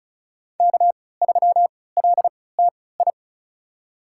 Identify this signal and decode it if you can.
Morse code, heard as a keyed tone, about 34 wpm, spaced as for 20 wpm, 700 Hz. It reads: K3LTI
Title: USA Call Signs - 3x Character Spacing - 35wpm